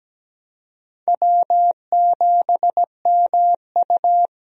Morse code, heard as a keyed tone, about 17 wpm, 700 Hz. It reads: W7MU
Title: USA Call Signs - 4 Characters - Repeated 2x - 17wpm